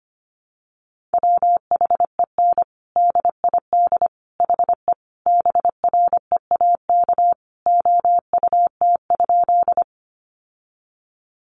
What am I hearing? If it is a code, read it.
Morse code, heard as a keyed tone, about 25 wpm, 700 Hz. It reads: WHEN DID HE BREAK OUT?
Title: Sentences from Top 900 Words 25wpm